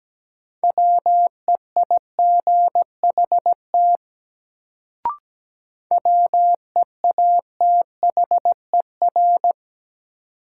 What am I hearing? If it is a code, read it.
Morse code, heard as a keyed tone, about 17 wpm, 700 Hz. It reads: WEIGHT WEATHER
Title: Sentences from Top 800 Words - New Word Drill - 17wpm